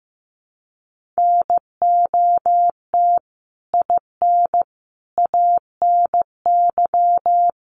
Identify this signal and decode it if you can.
Morse code, heard as a keyed tone, about 15 wpm, 700 Hz. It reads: NOT IN ANY
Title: Sentences from Top 1500 Words 15wpm